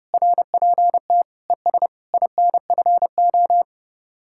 Morse code, heard as a keyed tone, about 37 wpm, 700 Hz. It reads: RPT ES INFO
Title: Speed-Racing Common QSO Phrases - 30wpm